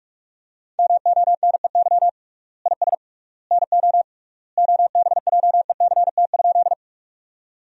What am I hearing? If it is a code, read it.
Morse code, heard as a keyed tone, about 42 wpm, 700 Hz. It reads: MONEY IS NO OBJECT?
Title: 3x Word-Spacing - Sentences from Top 500 Words - 45wpm